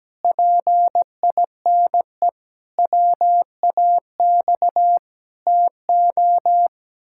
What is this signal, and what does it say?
Morse code, heard as a keyed tone, about 17 wpm, 700 Hz. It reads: PINE WAX TO
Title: Pangram Challenge v2 - 17wpm